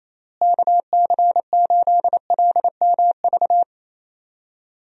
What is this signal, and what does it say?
Morse code, heard as a keyed tone, about 28 wpm, 700 Hz. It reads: KC8LMV
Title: USA Call Signs - 6 Characters - Repeated 2x - 28wpm